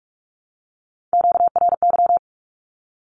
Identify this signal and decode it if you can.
Morse code, heard as a keyed tone, about 44 wpm, 700 Hz. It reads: QRY
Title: Q-Codes 45wpm